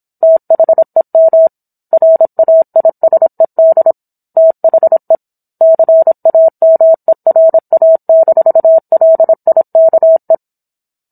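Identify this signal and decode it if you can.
Morse code, heard as a keyed tone, about 27 wpm, 650 Hz. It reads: THEM RAISED THE CAMERA-LIKE
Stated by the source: Morse code practice of a book chapter at 26 WPM.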